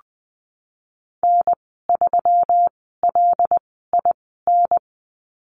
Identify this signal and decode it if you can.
Morse code, heard as a keyed tone, about 20 wpm, spaced as for 14 wpm, 700 Hz. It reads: N3LIN
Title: 128 Lesson 21g - Number 6 - Call Signs 20wpm